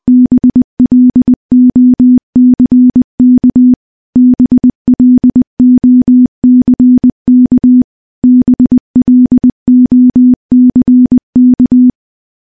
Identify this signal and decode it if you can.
Morse code, heard as a keyed tone, about 21 wpm, 250 Hz. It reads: BLOCK BLOCK BLOCK
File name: morse / morse-training-20WPM